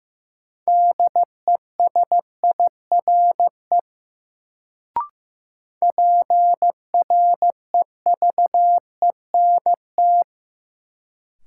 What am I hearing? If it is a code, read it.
Morse code, heard as a keyed tone, about 15 wpm, 700 Hz. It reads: DESIRE PREVENT
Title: Sentences from Top 1100 Words - New Word Drill - 15wpm